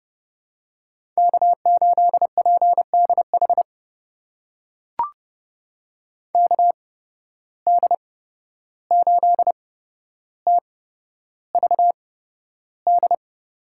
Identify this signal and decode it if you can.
Morse code, heard as a keyed tone, about 32 wpm, spaced as for 12 wpm, 700 Hz. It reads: K8PDH KD8TVD
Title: USA Call Signs - 8x Character Spacing - 30wpm